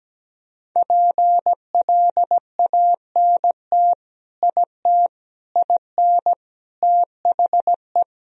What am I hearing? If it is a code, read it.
Morse code, heard as a keyed tone, about 17 wpm, 700 Hz. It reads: PLANT IT IN THE
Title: Sentences from Top 1000 Words - Review Part 2 - 17wpm